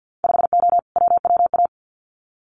Morse code, heard as a keyed tone, about 50 wpm, 700 Hz. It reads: 5O WWA
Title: Winter Field Day Exchanges Encore 50wpm